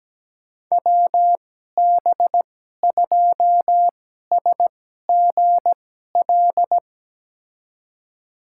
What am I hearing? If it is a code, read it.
Morse code, heard as a keyed tone, about 17 wpm, spaced as for 12 wpm, 700 Hz. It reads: WB2SGL
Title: USA Call Signs - 6 Characters - 2x Character Spacing - 17wpm